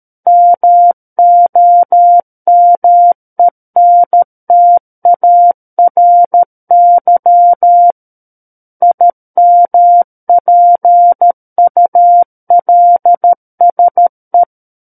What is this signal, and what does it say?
Morse code, heard as a keyed tone, about 13 wpm, 700 Hz. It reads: MOMENTARY IMPULSE
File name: Book One / 04-Chapter 4 - 13 WPM